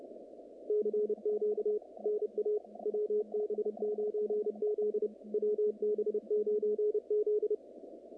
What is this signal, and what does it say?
Morse code, heard as a keyed tone, about 29 wpm, 440 Hz, with radio noise. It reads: CQ NA WB9Z WB9Z
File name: contests / naqp-cw-k5tr-op-k5pi- -2200